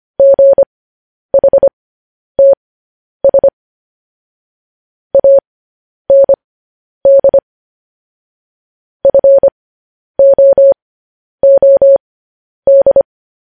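Morse code, heard as a keyed tone, about 25 wpm, spaced as for 10 wpm, 550 Hz. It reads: GHTS AND FOOD
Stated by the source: Morse code transmission of news headlines.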